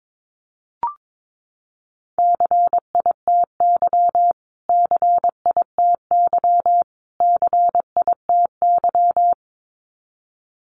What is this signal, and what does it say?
Morse code, heard as a keyed tone, about 22 wpm, 700 Hz. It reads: CITY CITY CITY
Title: Top 500 Words Repeated 22wpm